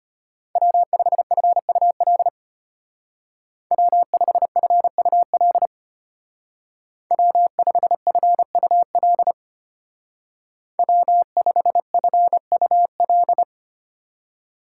Morse code, heard as a keyed tone, about 34 wpm, 700 Hz. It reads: W5FUL W5FUL W5FUL W5FUL